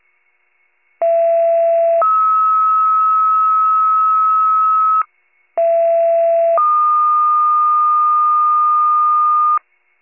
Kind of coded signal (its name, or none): two-tone page